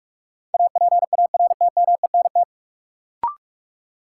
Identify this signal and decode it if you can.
Morse code, heard as a keyed tone, about 43 wpm, 700 Hz. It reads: APARTMENT
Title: Sentences from Top 900 Words - New Word Drill - 45wpm